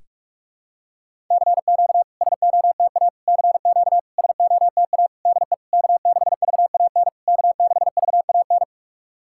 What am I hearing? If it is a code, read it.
Morse code, heard as a keyed tone, about 44 wpm, 660 Hz. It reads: CQ SOTA CQ SOTA DE K6VAN K6VAN
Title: SOTA QSO Practice 45wpm